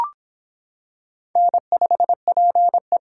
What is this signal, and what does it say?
Morse code, heard as a keyed tone, about 25 wpm, 700 Hz. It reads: N5PE